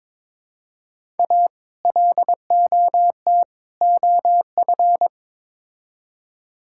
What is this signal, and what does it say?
Morse code, heard as a keyed tone, about 22 wpm, 700 Hz. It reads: A LOT OF
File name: practice / 22wpm / Build A Phrase - 5-Word Phrases - 22wpm